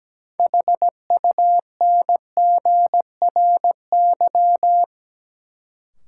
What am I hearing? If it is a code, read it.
Morse code, heard as a keyed tone, about 17 wpm, 700 Hz. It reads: HUNGRY
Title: Sentences from Top 1400 Words - New Word Drill Alt - 17wpm